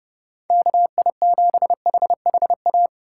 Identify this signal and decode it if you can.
Morse code, heard as a keyed tone, about 29 wpm, 700 Hz. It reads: KI7HHA
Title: USA Call Signs - 2x Character Spacing - 30wpm